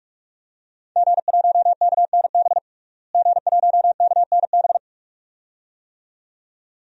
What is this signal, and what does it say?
Morse code, heard as a keyed tone, about 46 wpm, 700 Hz. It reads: G1KNB G1KNB
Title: DXCC-100 Call Signs - 5 Characters - Repeated 2x - 45wpm